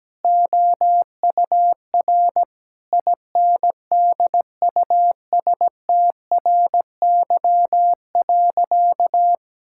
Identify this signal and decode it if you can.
Morse code, heard as a keyed tone, about 17 wpm, 700 Hz. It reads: OUR INDUSTRY.
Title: Sentences from Top 1000 Words - Review Part 6 - 17wpm